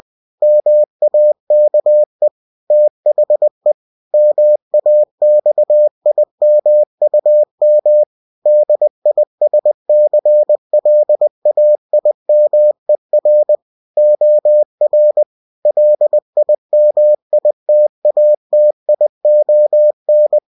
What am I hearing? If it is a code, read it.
Morse code, heard as a keyed tone, about 20 wpm, 600 Hz. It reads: MAKE THE MAXIMUM DISCLAIMER OR LIMITATION
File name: cw-books / winnie-the-pooh / winnie-20wpm-page-0119